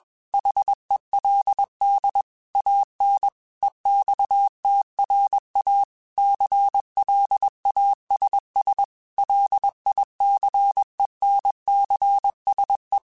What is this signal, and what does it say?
Morse code, heard as a keyed tone, about 21 wpm, 800 Hz. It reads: HELD AN EXTRA CLASS LICENCSE
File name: QSOs / QSOs-mp3s / kc7sac@20